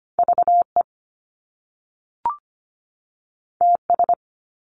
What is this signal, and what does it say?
Morse code, heard as a keyed tone, about 25 wpm, 700 Hz. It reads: VE TS
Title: Top Two Letter Combinations in Words 25wpm